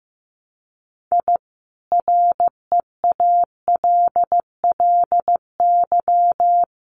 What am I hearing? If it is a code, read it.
Morse code, heard as a keyed tone, about 15 wpm, 700 Hz. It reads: I REALLY
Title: Sets of 5 Words Encore2 15wpm